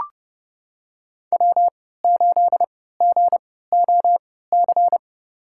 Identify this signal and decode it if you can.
Morse code, heard as a keyed tone, about 30 wpm, spaced as for 17 wpm, 700 Hz. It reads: W8GOC
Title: USA Call Signs - 5 Characters - 3x Character Spacing - 30wpm